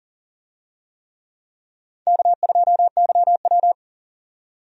Morse code, heard as a keyed tone, about 40 wpm, 700 Hz. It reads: K2YW
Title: USA Call Signs - 4 Characters - 40wpm